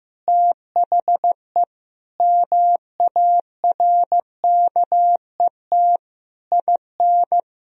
Morse code, heard as a keyed tone, about 16 wpm, 700 Hz. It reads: THE MARKET IN
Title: Sentences from Top 1000 Words - Review Part 3 - 15wpm